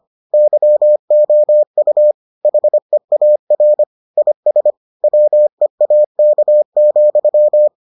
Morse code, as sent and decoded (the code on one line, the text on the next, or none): -.-- --- ..- / .... .- .-. / .. ... / .-- . .- -.- --..--
YOU HEAR IS WEAK,